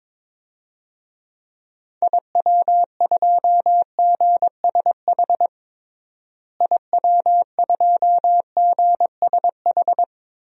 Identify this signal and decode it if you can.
Morse code, heard as a keyed tone, about 22 wpm, 700 Hz. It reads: IW2GSH IW2GSH
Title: DXCC-100 Call Signs - 6 Characters - Repeated 3x - 22wpm